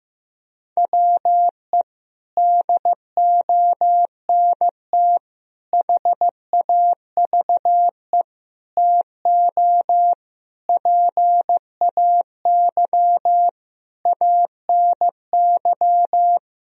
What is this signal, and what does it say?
Morse code, heard as a keyed tone, about 15 wpm, 700 Hz. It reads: WE DONT HAVE TO PAY ANY
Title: Sentences from Top 1000 Words - Review Part 2 - 15wpm